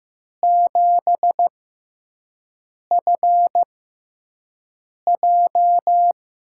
Morse code, heard as a keyed tone, about 15 wpm, spaced as for 5 wpm, 700 Hz. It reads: 7FJ
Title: USA Call Signs - 6 Characters - 6x Character Spacing - 15wpm